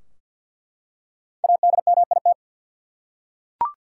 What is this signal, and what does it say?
Morse code, heard as a keyed tone, about 47 wpm, 700 Hz. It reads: ADMIT
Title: Five Letter Words - 50wpm